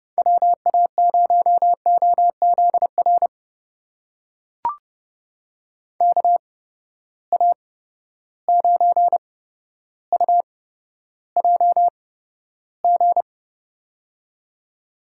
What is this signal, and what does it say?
Morse code, heard as a keyed tone, about 30 wpm, spaced as for 13 wpm, 700 Hz. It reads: WA0OZR KA9UJG
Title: USA Call Signs - 6 Characters - 8x Character Spacing - 30wpm